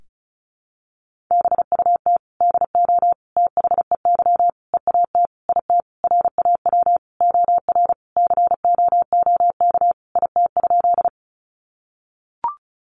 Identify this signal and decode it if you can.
Morse code, heard as a keyed tone, about 34 wpm, 700 Hz. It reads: BUT DO THEY EAT IT RAW OR COOK IT?
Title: Sentences from Top 1500 Words 35wpm